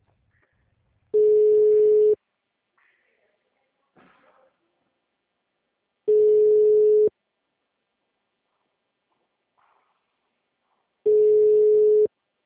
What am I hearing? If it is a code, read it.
Ringback tone (Europe, 425 Hz).